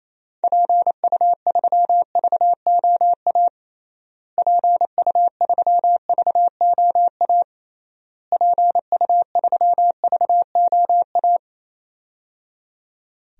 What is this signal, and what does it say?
Morse code, heard as a keyed tone, about 28 wpm, 700 Hz. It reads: PU3VOA PU3VOA PU3VOA
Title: DXCC-100 Call Signs - 6 Characters - Repeated 3x - 28wpm